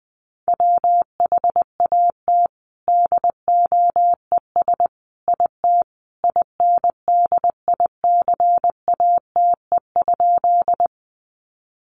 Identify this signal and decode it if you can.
Morse code, heard as a keyed tone, about 20 wpm, 700 Hz. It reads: WHAT DOES IT INDICATE?
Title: Sentences from Top 1300 Words 20wpm